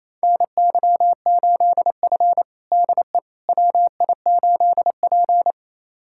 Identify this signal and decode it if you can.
Morse code, heard as a keyed tone, about 28 wpm, 700 Hz. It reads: NY8F DE WI8P
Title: USA Callsign DE Callsign - 4 Characters - 28wpm